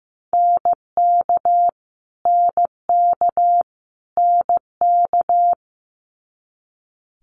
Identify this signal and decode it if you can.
Morse code, heard as a keyed tone, about 15 wpm, 700 Hz. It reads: NK NK NK
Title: Multi-letter Phonemes Repeated 3x 15wpm